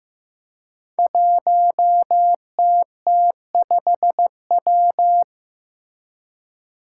Morse code, heard as a keyed tone, about 15 wpm, 700 Hz. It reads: 1TT5W
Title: Five Challenging Letter-Number Combinations 15wpm